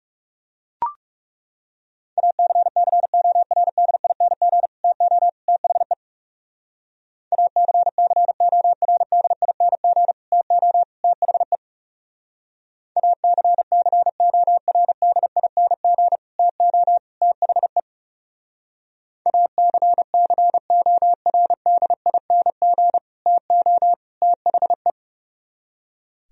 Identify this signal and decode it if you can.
Morse code, heard as a keyed tone, about 38 wpm, 700 Hz. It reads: ACCORDING TO THE ACCORDING TO THE ACCORDING TO THE ACCORDING TO THE
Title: Speed-Racing - Sets of 3 Words 30wpm